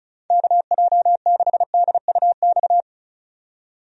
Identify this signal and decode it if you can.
Morse code, heard as a keyed tone, about 36 wpm, 700 Hz. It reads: KJ6DUX
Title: USA Call Signs - 6 Characters - 35wpm